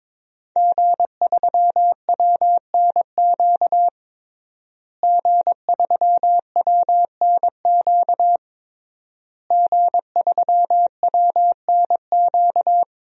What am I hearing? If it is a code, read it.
Morse code, heard as a keyed tone, about 22 wpm, 700 Hz. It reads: G3WNQ G3WNQ G3WNQ
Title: DXCC-100 Call Signs - 5 Characters - Repeated 3x - 22wpm